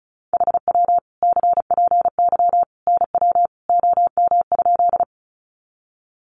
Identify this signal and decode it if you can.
Morse code, heard as a keyed tone, about 34 wpm, 700 Hz. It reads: HW CPY NW OM?
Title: Common QSO Phrases 35wpm